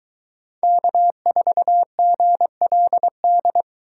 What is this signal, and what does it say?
Morse code, heard as a keyed tone, about 22 wpm, 700 Hz. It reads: K4GLD